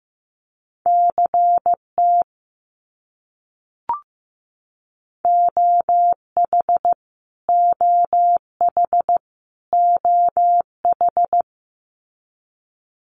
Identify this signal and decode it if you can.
Morse code, heard as a keyed tone, about 15 wpm, 700 Hz. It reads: CT OH OH OH
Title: US States Repeated 3x 15wpm